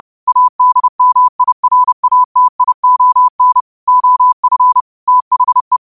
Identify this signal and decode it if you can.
Morse code, heard as a keyed tone, about 30 wpm, 1000 Hz. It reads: ADMIRATION OF THE